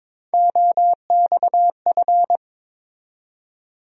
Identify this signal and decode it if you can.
Morse code, heard as a keyed tone, about 22 wpm, 700 Hz. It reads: OXF
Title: USA Call Sign Postfixes - 3 Letters 22wpm